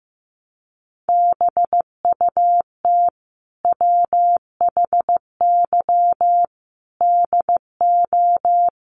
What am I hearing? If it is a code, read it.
Morse code, heard as a keyed tone, about 15 wpm, 700 Hz. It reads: BUT WHY DO
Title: Sentences from Top 1200 Words 15wpm